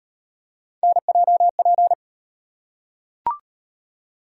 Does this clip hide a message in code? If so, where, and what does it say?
Yes: Morse code from 0.8 to 1.9 s, heard as a keyed tone, about 38 wpm, 700 Hz. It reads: NJP